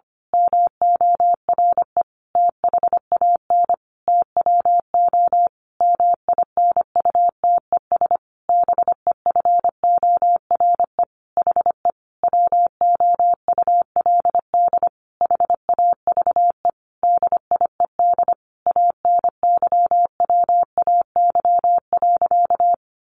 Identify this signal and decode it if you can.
Morse code, heard as a keyed tone, about 25 wpm, 700 Hz. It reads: MORE THAN TWO MINUTES BEFORE HE WOULD HAVE DIED ANYWAY.
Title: A Sound Of Thunder 25wpm